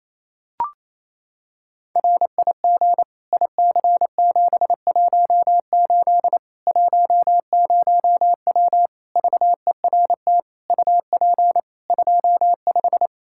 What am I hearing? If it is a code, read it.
Morse code, heard as a keyed tone, about 28 wpm, 700 Hz. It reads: RIG IC718 10W VERT UP 25
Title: Practice QSO - Exchange 2 - Info - 28wpm